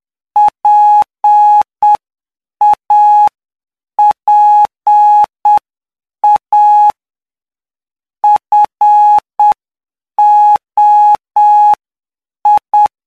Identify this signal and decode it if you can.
Morse code, heard as a keyed tone, about 7 wpm, spaced as for 6 wpm, 840 Hz. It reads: PAPA FOI
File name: cw / py2lcd / 15-PY2LCD-Li��o 24